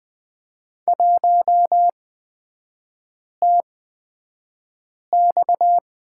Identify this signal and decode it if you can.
Morse code, heard as a keyed tone, about 20 wpm, spaced as for 5 wpm, 700 Hz. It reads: 1TX
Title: 250 Lesson 40e - BK - Three Characters 20wpm